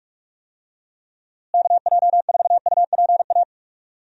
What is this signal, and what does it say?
Morse code, heard as a keyed tone, about 43 wpm, 700 Hz. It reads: KJ4UPA
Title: USA Call Signs - 6 Characters - Repeated 2x - 45wpm